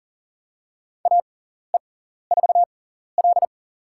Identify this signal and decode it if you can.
Morse code, heard as a keyed tone, about 37 wpm, spaced as for 14 wpm, 700 Hz. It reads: AE4L
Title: USA Call Signs - 4 Characters - 6x Character Spacing - 40wpm